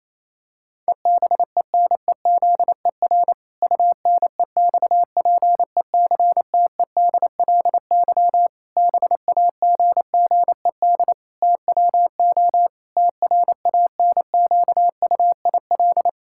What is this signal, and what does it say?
Morse code, heard as a keyed tone, about 27 wpm, 700 Hz. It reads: EBENEZER UNEXPECTEDLY BAGGED TWO TRANQUIL
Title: Pangram Challenge v2 - 28wpm